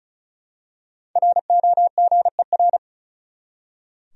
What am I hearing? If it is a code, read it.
Morse code, heard as a keyed tone, about 36 wpm, 700 Hz. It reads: ROGER
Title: USA Common Names - Max 5 Letters - 35wpm